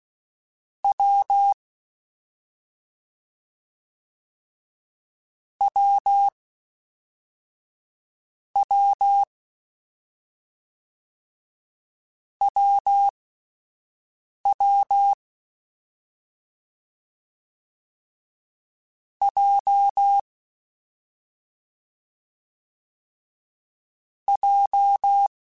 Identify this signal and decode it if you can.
Morse code, heard as a keyed tone, about 16 wpm, spaced as for 4 wpm, 800 Hz. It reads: W WW WW J J
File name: Difficult Combos / Difficult Combos-mp3s / AWJ ShortRun @5